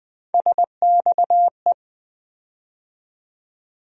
Morse code, heard as a keyed tone, about 20 wpm, 700 Hz. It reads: SXE
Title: USA Call Sign Postfixes - 3 Letters 20wpm